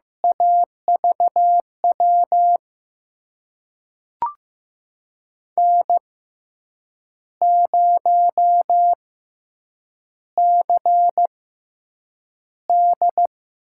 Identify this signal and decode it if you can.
Morse code, heard as a keyed tone, about 15 wpm, spaced as for 7 wpm, 700 Hz. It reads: AVW N0CD
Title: USA Call Signs - 6x Character Spacing - 15wpm